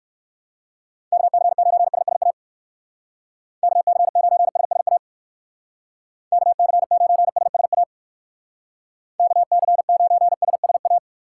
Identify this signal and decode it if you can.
Morse code, heard as a keyed tone, about 56 wpm, 700 Hz. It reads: KC9SSA KC9SSA KC9SSA KC9SSA
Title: Speed Racing - USA Call Signs 45wpm